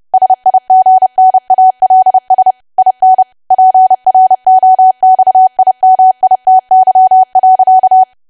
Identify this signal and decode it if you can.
Morse code, heard as a keyed tone, about 30 wpm, 750 Hz. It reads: SIGNALS IN PROXIMITY.
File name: cwa / W1AW-Level3 / 30WPM